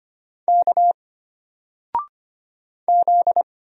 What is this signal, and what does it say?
Morse code, heard as a keyed tone, about 25 wpm, 700 Hz. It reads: K Z
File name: Single-Letter-Number-25wpm